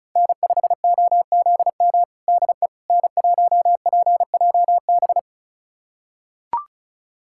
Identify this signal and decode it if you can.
Morse code, heard as a keyed tone, about 36 wpm, 700 Hz. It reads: N5OZM DE N1PJB
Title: USA Callsign DE Callsign - 5 Characters - 35wpm